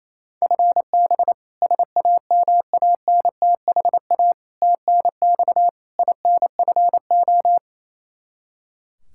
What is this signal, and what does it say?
Morse code, heard as a keyed tone, about 28 wpm, 700 Hz. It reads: FB SAMANTHA TNX INFO
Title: K1USN SST QSO Practice 28wpm